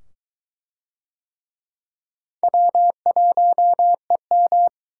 Morse code, heard as a keyed tone, about 22 wpm, 700 Hz. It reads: W1EM